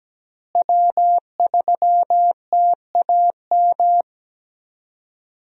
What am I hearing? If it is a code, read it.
Morse code, heard as a keyed tone, about 17 wpm, 700 Hz. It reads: W3TAM
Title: USA Call Signs - 5 Characters - Repeated 3x - 17wpm